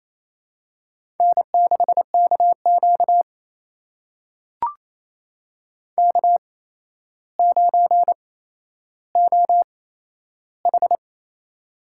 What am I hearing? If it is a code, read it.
Morse code, heard as a keyed tone, about 28 wpm, spaced as for 12 wpm, 700 Hz. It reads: N6KQ K9OH
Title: USA Call Signs - 4 Characters - 8x Character Spacing - 28wpm